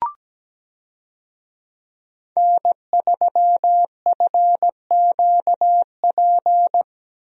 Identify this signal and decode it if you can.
Morse code, heard as a keyed tone, about 17 wpm, 700 Hz. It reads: N3FQP
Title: USA Call Signs - 5 Characters - 17wpm